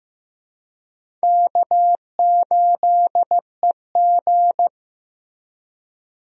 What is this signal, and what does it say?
Morse code, heard as a keyed tone, about 15 wpm, 700 Hz. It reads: K8EG
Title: USA Call Signs - 4 Characters - Repeated 3x - 15wpm